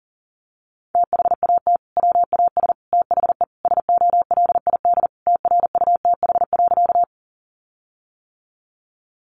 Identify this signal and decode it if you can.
Morse code, heard as a keyed tone, about 40 wpm, 700 Hz. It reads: THAT WAS THE SOLID TRUTH.
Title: Sentences from Top 1500 Words 40wpm